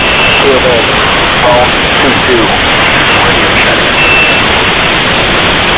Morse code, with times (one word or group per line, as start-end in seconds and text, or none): none